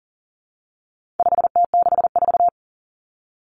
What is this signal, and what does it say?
Morse code, heard as a keyed tone, about 40 wpm, 700 Hz. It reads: 5T64
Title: Common Cut Numbers - 4-Digit 40wpm